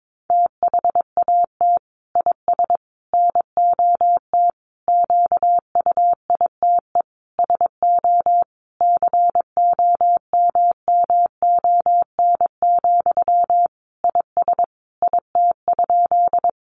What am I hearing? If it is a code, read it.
Morse code, heard as a keyed tone, about 22 wpm, 700 Hz. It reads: THAT IS NOT QUITE SO COMMON, IS IT?
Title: Sentences from Top 700 Words 22wpm